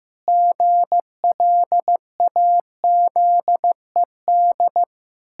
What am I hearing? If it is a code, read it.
Morse code, heard as a keyed tone, about 15 wpm, 700 Hz. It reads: GLAZED
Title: Pangram Challenge v2 - 15wpm